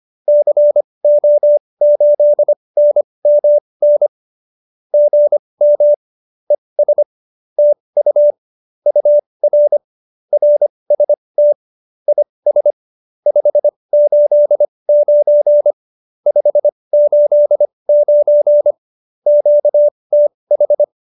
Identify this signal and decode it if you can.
Morse code, heard as a keyed tone, about 25 wpm, spaced as for 20 wpm, 590 Hz. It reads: CO8NMN GM ES TU UR RST IS 589 589 QTH